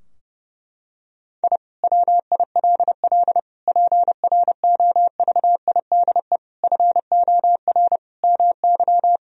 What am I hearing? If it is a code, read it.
Morse code, heard as a keyed tone, about 30 wpm, 700 Hz. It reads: I WILL PROVIDE FOR MY
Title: Sentences from Top 1000 Words - Review Part 3 - 30wpm